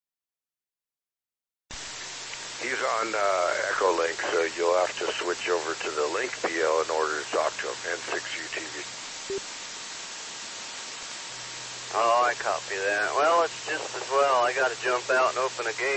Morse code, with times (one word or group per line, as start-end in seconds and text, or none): none